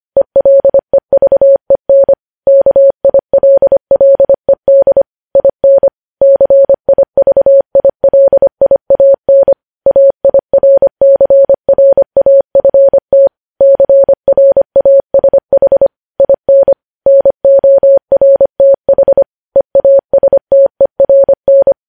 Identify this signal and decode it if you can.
Morse code, heard as a keyed tone, about 25 wpm, 550 Hz. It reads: ELEVEN KILLED IN CIVILIAN AIRCRAFT CRASH IN NORTH EASTERN
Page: news-headlines-in-morse-code-at-25-wpm